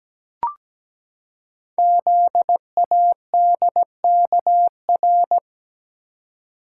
Morse code, heard as a keyed tone, about 17 wpm, 700 Hz. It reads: ZADKR
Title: Code Groups - 5 Letters - 17wpm